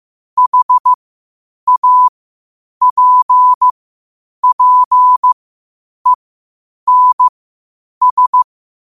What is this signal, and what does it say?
Morse code, heard as a keyed tone, about 15 wpm, spaced as for 8 wpm, 1000 Hz. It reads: HAPPENS
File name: qotdfiles / 8wpm